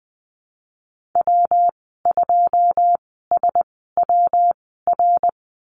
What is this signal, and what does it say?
Morse code, heard as a keyed tone, about 20 wpm, spaced as for 15 wpm, 700 Hz. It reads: W2SWR
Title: 161 Lesson 26g - Letter g - Call Signs 20wpm